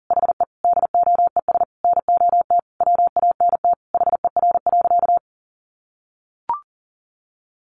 Morse code, heard as a keyed tone, about 40 wpm, 690 Hz. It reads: HE DOES NOT WANT HER.
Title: Sentences from Top 800 Words 40wpm